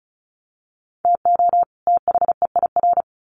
Morse code, heard as a keyed tone, about 34 wpm, 700 Hz. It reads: TO THEIR
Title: Sets of 2 Words 35wpm